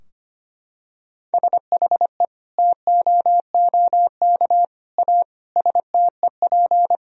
Morse code, heard as a keyed tone, about 25 wpm, 700 Hz. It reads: SHE TOOK A STEP
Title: Sentences from Top 1000 Words - Review Part 2 - 25wpm